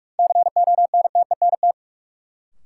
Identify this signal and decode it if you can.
Morse code, heard as a keyed tone, about 46 wpm, 700 Hz. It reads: CONTENT
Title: Sentences from Top 900 Words - New Word Drill - 45wpm